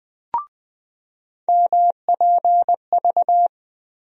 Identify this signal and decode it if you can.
Morse code, heard as a keyed tone, about 21 wpm, 700 Hz. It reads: MPV
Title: USA Call Sign Postfixes - 3 Letters 20wpm